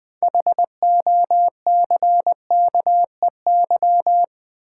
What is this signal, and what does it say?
Morse code, heard as a keyed tone, about 20 wpm, 700 Hz. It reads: HOCKEY
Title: 6 Letter Words - First 3 Letters Given - 20wpm